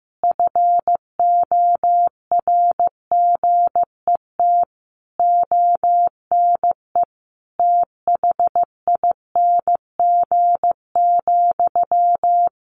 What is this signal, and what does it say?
Morse code, heard as a keyed tone, about 15 wpm, 700 Hz. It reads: FORGET ONE THING,
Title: Sentences from Top 1200 Words 15wpm